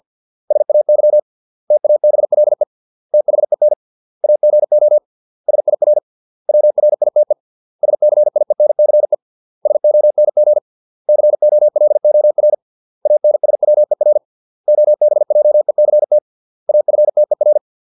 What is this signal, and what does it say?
Morse code, heard as a keyed tone, about 50 wpm, 600 Hz. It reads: SAY TABLE THEN AGO SIR WRITE SCIENCE SONG COLOR ANSWER OBJECT AFTER